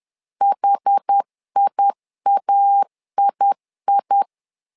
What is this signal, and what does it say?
Morse code, heard as a keyed tone, about 10 wpm, 800 Hz. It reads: HIAII
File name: LEZIONI / 2 UHDN / 2 UH 10wpm 1714